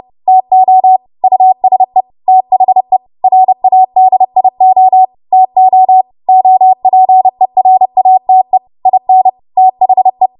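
Morse code, heard as a keyed tone, about 30 wpm, 750 Hz. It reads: TO USE THE RADIO TO OPERATE IN THE
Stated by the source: Morse practice transmission at 30 WPM.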